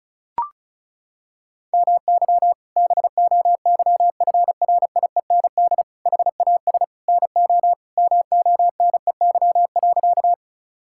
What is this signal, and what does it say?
Morse code, heard as a keyed tone, about 36 wpm, 700 Hz. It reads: MY BOYFRIEND HAS NO MONEY.
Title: Sentences from Top 1000 Words - Review Part 5 - 35wpm